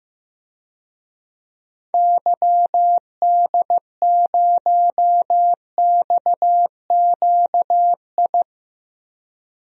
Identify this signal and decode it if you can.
Morse code, heard as a keyed tone, about 15 wpm, 700 Hz. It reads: YD0XQI
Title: DXCC-100 Call Signs - 6 Characters - Repeated 2x - 15wpm